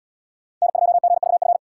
Morse code, heard as a keyed tone, about 68 wpm, 700 Hz. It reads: N2GVF